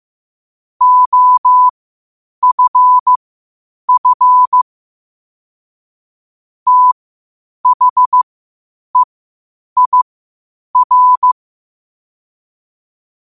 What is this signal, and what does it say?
Morse code, heard as a keyed tone, about 15 wpm, spaced as for 8 wpm, 1000 Hz. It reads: OFF THEIR
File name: qotdfiles / 8wpm